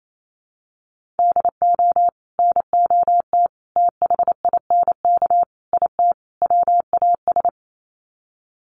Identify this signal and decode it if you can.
Morse code, heard as a keyed tone, about 28 wpm, 700 Hz. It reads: DO NOT THINK IT WAS
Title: Sets of 5 Words Encore 28wpm